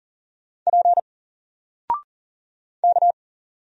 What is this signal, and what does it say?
Morse code, heard as a keyed tone, about 40 wpm, 700 Hz. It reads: P K
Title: Single Letter-Number 40wpm